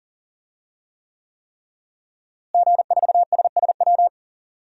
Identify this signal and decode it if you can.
Morse code, heard as a keyed tone, about 40 wpm, 700 Hz. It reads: G4SSW
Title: DXCC-100 Call Signs - 5 Characters - Repeated 3x - 40wpm